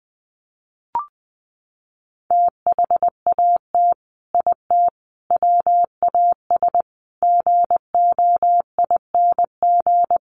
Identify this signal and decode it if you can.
Morse code, heard as a keyed tone, about 20 wpm, 700 Hz. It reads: THAT IT WAS GOING
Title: Sets of 5 Words Encore2 20wpm